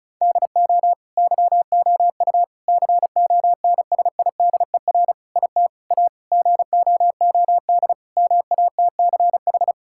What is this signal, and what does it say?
Morse code, heard as a keyed tone, about 36 wpm, 700 Hz. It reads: DO YOU CONSIDER IT A GOOD MATCH
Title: Sentences from Top 1600 Words 35wpm